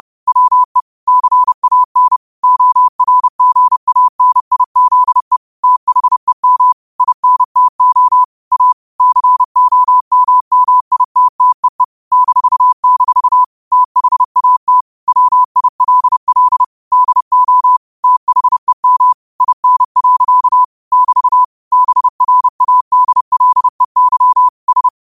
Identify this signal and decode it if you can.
Morse code, heard as a keyed tone, about 30 wpm, 1000 Hz. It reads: WE CAN ORGANIZE THEM INTO A COMMITTEE -- THAT WILL DO THEM IN. <BT> BRADLEY S
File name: qotdfiles / 30wpm